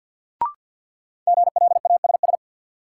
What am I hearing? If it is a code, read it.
Morse code, heard as a keyed tone, about 52 wpm, 700 Hz. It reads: GLASS
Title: Sentences from Top 1300 Words - New Word Drill Alt - 50wpm